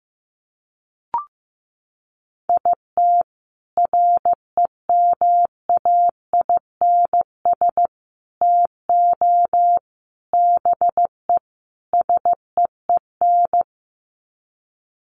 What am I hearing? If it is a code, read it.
Morse code, heard as a keyed tone, about 15 wpm, 700 Hz. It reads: IT REMAINS TO BE SEEN
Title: Sets of 5 Words Encore 15wpm